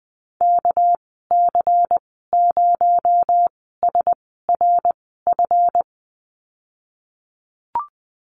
Morse code, heard as a keyed tone, about 20 wpm, spaced as for 15 wpm, 700 Hz. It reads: KC0SRF
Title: 238 Lesson 38g - Letter q - Call Signs 20wpm